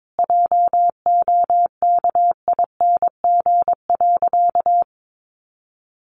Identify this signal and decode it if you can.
Morse code, heard as a keyed tone, about 22 wpm, 700 Hz. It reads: JOKING.
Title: A Sound Of Thunder 22wpm